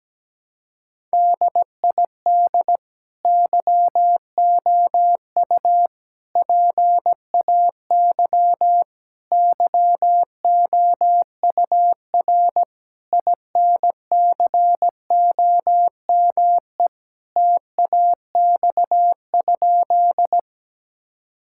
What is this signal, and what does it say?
Morse code, heard as a keyed tone, about 17 wpm, 700 Hz. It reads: DID YOU PAY YOUR INCOME TAX?
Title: Sentences from Top 1000 Words - Review Part 5 - 17wpm